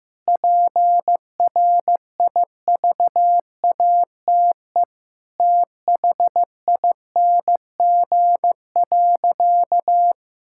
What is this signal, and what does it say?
Morse code, heard as a keyed tone, about 15 wpm, 700 Hz. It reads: PRIVATE THING.
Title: Sentences from Top 1600 Words 15wpm